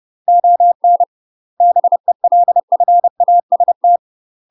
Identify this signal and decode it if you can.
Morse code, heard as a keyed tone, about 30 wpm, 700 Hz. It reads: ON BELFAST
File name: UK MASTER QSOs TRAINING / 30WPM / UK MASTER QSOs 0018